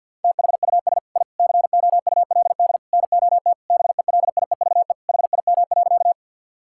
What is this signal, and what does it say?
Morse code, heard as a keyed tone, about 50 wpm, 700 Hz. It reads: THUS I COULD NOT BELIEVE HIM.
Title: Sentences from Top 1000 Words - Review Part 5 - 50wpm